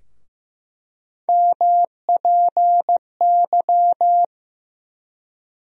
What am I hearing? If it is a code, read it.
Morse code, heard as a keyed tone, about 15 wpm, 700 Hz. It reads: MPY
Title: USA Call Sign Postfixes - 3 Letters 15wpm